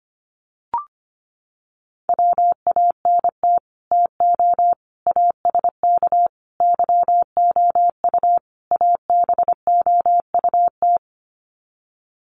Morse code, heard as a keyed tone, about 25 wpm, 700 Hz. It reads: WANT TO ASK YOU ABOUT
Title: Sets of 5 Words Encore2 25wpm